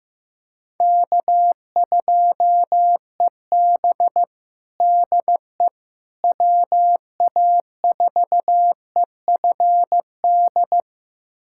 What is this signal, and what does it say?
Morse code, heard as a keyed tone, about 15 wpm, 700 Hz. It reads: K2EB DE WA4EFD
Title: USA Callsign DE Callsign 15wpm